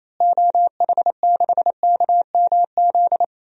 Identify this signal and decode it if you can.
Morse code, heard as a keyed tone, about 27 wpm, 700 Hz. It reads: OH6KMZ
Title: DXCC-100 Call Signs - 6 Characters - Repeated 3x - 28wpm